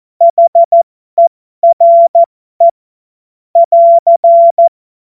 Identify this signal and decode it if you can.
Morse code, heard as a keyed tone, about 14 wpm, spaced as for 12 wpm, 680 Hz. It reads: HERE <AR>